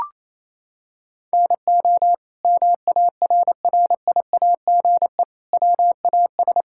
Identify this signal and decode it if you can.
Morse code, heard as a keyed tone, about 28 wpm, 700 Hz. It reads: NO MARRIAGE WAS
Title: Sentences from Top 1000 Words - Review Part 5 - 28wpm